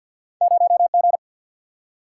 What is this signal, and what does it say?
Morse code, heard as a keyed tone, about 53 wpm, 700 Hz. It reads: 0 G
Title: Single-Letter Single-Number 50wpm